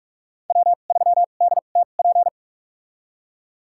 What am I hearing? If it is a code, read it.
Morse code, heard as a keyed tone, about 46 wpm, spaced as for 33 wpm, 700 Hz. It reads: W3DTP
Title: USA Call Signs - 5 Characters - 2x Character Spacing - 45wpm